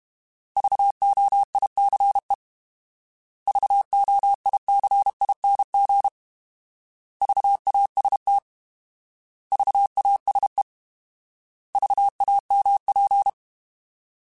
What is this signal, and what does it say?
Morse code, heard as a keyed tone, about 32 wpm, 800 Hz. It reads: VOICE VOICING VAST VASE VAMP
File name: Alphabetical / Alphabetical mp3s / Vs @30